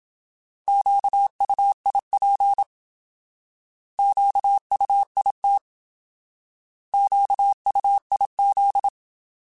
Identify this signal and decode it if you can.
Morse code, heard as a keyed tone, about 27 wpm, 800 Hz. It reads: QUIP QUIT QUIZ
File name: Alphabetical / Alphabetical mp3s / Qs @25